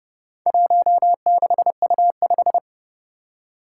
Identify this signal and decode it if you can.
Morse code, heard as a keyed tone, about 30 wpm, 700 Hz. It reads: 16U5
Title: Four Challenging Letter-Number Combinations 30wpm